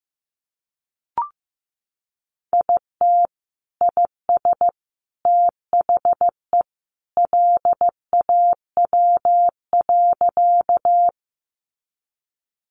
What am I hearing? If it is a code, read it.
Morse code, heard as a keyed tone, about 15 wpm, 700 Hz. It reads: IT IS THE LAW.
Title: Sentences from Top 600 Words 15wpm